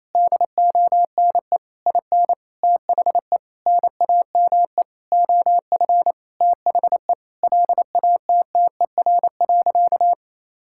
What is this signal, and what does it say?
Morse code, heard as a keyed tone, about 28 wpm, 700 Hz. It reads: DONE IN THE NAME OF THE LATTER.
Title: Sentences from Top 1600 Words 28wpm